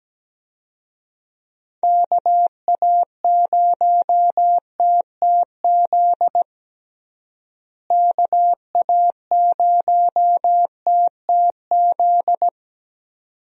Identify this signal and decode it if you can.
Morse code, heard as a keyed tone, about 17 wpm, 700 Hz. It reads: KA0TTZ KA0TTZ
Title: USA Call Signs - 6 Characters - Repeated 2x - 17wpm